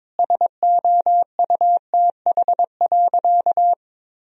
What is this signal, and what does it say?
Morse code, heard as a keyed tone, about 22 wpm, 700 Hz. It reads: SOUTH.
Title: Sentences from Top 1000 Words - Review Part 2 - 22wpm